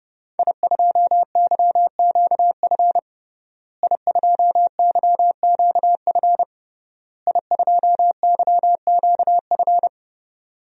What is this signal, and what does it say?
Morse code, heard as a keyed tone, about 30 wpm, 700 Hz. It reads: I2YQF I2YQF I2YQF
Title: DXCC-100 Call Signs - 5 Characters - Repeated 3x - 30wpm